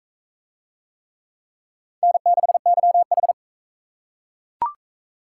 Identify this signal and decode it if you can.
Morse code, heard as a keyed tone, about 36 wpm, 700 Hz. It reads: N6YH